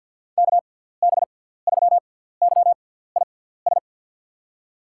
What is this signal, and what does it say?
Morse code, heard as a keyed tone, about 46 wpm, spaced as for 17 wpm, 700 Hz. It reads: KB3YIS
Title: USA Call Signs - 6 Characters - 6x Character Spacing - 50wpm